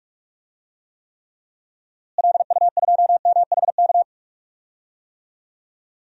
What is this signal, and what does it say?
Morse code, heard as a keyed tone, about 44 wpm, 700 Hz. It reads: LU2MHK
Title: DXCC-100 Call Signs - 6 Characters - Repeated 3x - 45wpm